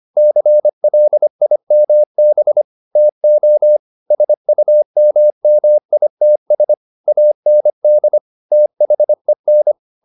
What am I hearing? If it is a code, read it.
Morse code, heard as a keyed tone, about 25 wpm, 600 Hz. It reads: CLIMB TO SUMMITS AND THEN